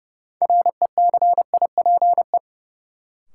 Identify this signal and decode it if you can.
Morse code, heard as a keyed tone, about 30 wpm, 700 Hz. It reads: RECIPE
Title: Sentences from Top 1200 Words - New Word Drill - 30wpm